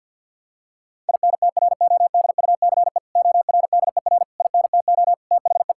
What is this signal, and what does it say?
Morse code, heard as a keyed tone, about 50 wpm, 700 Hz. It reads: INTRODUCE ORDER INTO THE
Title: Sentences from Top 1600 Words 50wpm